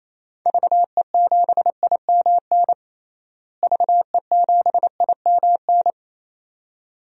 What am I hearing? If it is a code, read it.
Morse code, heard as a keyed tone, about 28 wpm, 700 Hz. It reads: VE7IMN VE7IMN
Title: DXCC-100 Call Signs - 6 Characters - Repeated 2x - 28wpm